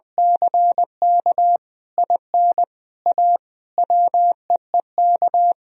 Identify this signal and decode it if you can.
Morse code, heard as a keyed tone, about 21 wpm, 700 Hz. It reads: CK IN A WEEK
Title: Sentences from Top 1000 Words - Review Part 2 - 20wpm